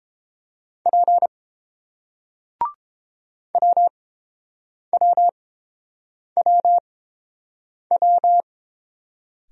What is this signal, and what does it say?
Morse code, heard as a keyed tone, about 30 wpm, spaced as for 7 wpm, 700 Hz. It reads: P WWWW